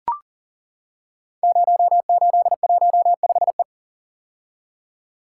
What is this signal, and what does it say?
Morse code, heard as a keyed tone, about 41 wpm, 700 Hz. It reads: 0815E
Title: Code Groups - 5 Letter:Numbers - 40wpm